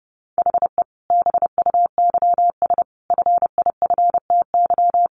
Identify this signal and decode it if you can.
Morse code, heard as a keyed tone, about 30 wpm, 700 Hz. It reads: HE BUYS FIFTY
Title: Additional Easy Phrases 30wpm